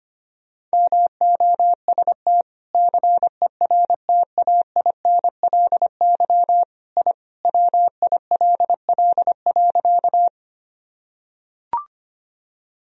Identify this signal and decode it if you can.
Morse code, heard as a keyed tone, about 25 wpm, 700 Hz. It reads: MOST CERTAINLY I WILL.
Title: Sentences from Top 1000 Words - Review Part 1 - 25wpm